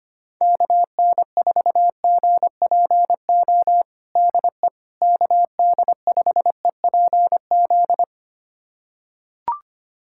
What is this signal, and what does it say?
Morse code, heard as a keyed tone, about 25 wpm, 700 Hz. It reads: KN4GPO DE KD5EPZ
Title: USA Callsign DE Callsign - 6 Characters - 25wpm